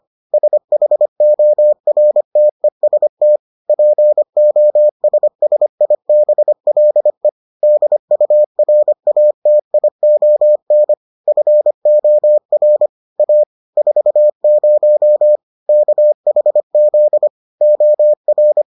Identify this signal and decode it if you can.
Morse code, heard as a keyed tone, about 25 wpm, 600 Hz. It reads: SHORTEST POSSIBLE DURATION FOR A 40 KHZ OR